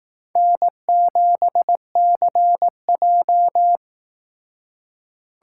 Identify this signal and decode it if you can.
Morse code, heard as a keyed tone, about 19 wpm, 700 Hz. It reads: N7CJ
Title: Speed Racing - USA 1x2 Call Signs 15wpm